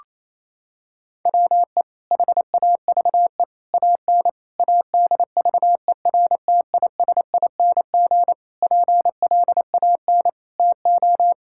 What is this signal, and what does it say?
Morse code, heard as a keyed tone, about 27 wpm, 700 Hz. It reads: WE HAVE AN ADVERTISING PLAN TO
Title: Sentences from Top 1000 Words - Review Part 6 - 28wpm